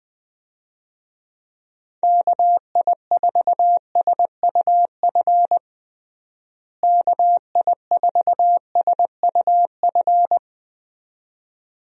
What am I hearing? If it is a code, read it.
Morse code, heard as a keyed tone, about 20 wpm, 700 Hz. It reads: KI4SUF KI4SUF
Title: USA Call Signs - 6 Characters - Repeated 2x - 20wpm